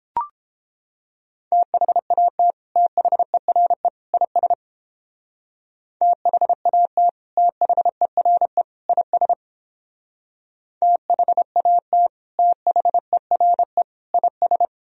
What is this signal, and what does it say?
Morse code, heard as a keyed tone, about 29 wpm, 700 Hz. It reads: THAT THERE IS THAT THERE IS THAT THERE IS